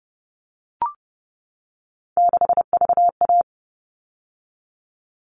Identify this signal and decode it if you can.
Morse code, heard as a keyed tone, about 30 wpm, 700 Hz. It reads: 6VA
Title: All Cut Numbers - 3-Digit 30wpm